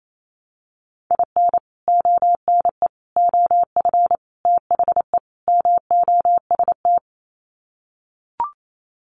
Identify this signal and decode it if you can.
Morse code, heard as a keyed tone, about 28 wpm, 700 Hz. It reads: IN ONE OF THE MOST
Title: Sets of 5 Words Encore 28wpm